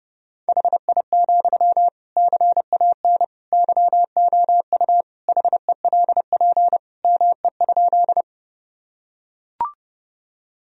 Morse code, heard as a keyed tone, about 30 wpm, 700 Hz. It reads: HI, CAN YOU HELP ME?
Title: Sentences from Top 1000 Words - Review Part 3 - 30wpm